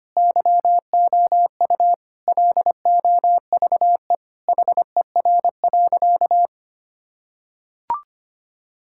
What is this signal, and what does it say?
Morse code, heard as a keyed tone, about 25 wpm, 700 Hz. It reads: YOU LOVE HER.
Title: Sentences from Top 1600 Words 25wpm